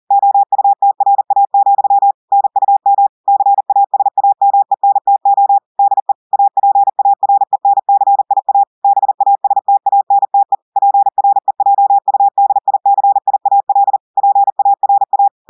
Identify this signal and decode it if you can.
Morse code, heard as a keyed tone, about 40 wpm, 800 Hz. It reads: OUTRA, NUM CASAMENTO DE APARENCIA BASTANTE PREJUDICIAL PARA
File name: audios / portugais / portugais texte en clair 1 vitesse 40 mots minutes